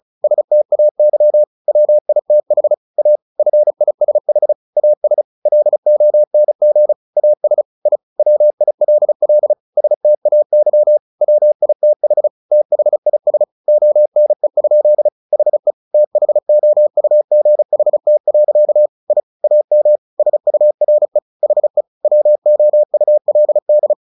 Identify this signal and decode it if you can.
Morse code, heard as a keyed tone, about 34 wpm, 600 Hz. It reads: STAY WITH A FISH AS LONG AS I WILL STAY WITH THIS ONE? HE THOUGHT. I AM SURE HE WOULD
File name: the old man and the sea / the old man and the sea 1 35wpm